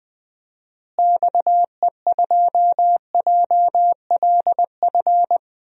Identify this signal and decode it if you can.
Morse code, heard as a keyed tone, about 20 wpm, 700 Hz. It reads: XE2JLF
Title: DXCC-100 Call Signs - 6 Characters - Repeated 3x - 20wpm